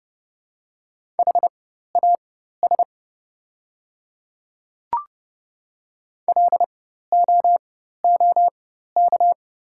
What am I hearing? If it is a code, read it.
Morse code, heard as a keyed tone, about 30 wpm, spaced as for 14 wpm, 690 Hz. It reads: HAS LOOK
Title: 4x Character Spacing - Top 100 Words - 30wpm